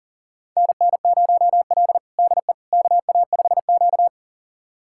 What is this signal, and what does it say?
Morse code, heard as a keyed tone, about 40 wpm, 700 Hz. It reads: NN0L DE KA5Q
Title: USA Callsign DE Callsign - 4 Characters - 40wpm